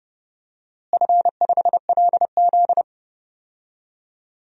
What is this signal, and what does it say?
Morse code, heard as a keyed tone, about 31 wpm, 700 Hz. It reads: F5LZ
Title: DXCC-100 Call Signs - 4 Characters - Repeated 2x - 30wpm